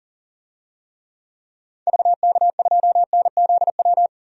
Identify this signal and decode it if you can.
Morse code, heard as a keyed tone, about 39 wpm, 700 Hz. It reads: VK2NZW
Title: DXCC-100 Call Signs - 6 Characters - 40wpm